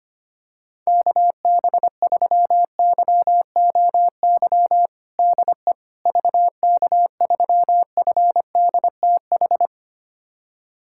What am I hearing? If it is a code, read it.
Morse code, heard as a keyed tone, about 25 wpm, 700 Hz. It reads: KB3YOY DE VK3FDTH
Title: USA Callsign DE Callsign 25wpm